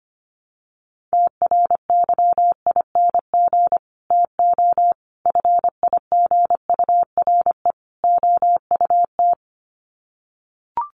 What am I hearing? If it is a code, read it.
Morse code, heard as a keyed tone, about 25 wpm, 700 Hz. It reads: TRYING TO FIGURE OUT
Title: Sets of 4 Words 25wpm